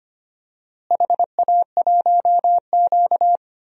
Morse code, heard as a keyed tone, about 25 wpm, 700 Hz. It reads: HA1Q
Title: DXCC-100 Call Signs - 4 Characters - Repeated 3x - 25wpm